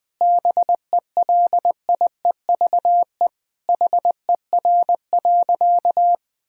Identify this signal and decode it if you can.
Morse code, heard as a keyed tone, about 20 wpm, 700 Hz. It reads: BELIEVE HER.
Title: Sentences from Top 1000 Words - Review Part 4 - 20wpm